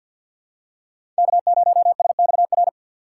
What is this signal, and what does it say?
Morse code, heard as a keyed tone, about 47 wpm, 700 Hz. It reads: K0SXR
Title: USA Call Signs - 5 Characters - 8x Character Spacing - 50wpm